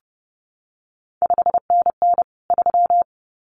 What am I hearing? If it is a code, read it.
Morse code, heard as a keyed tone, about 30 wpm, 700 Hz. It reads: 5NN 3
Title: CQ WW DX Contest Exchange - 30wpm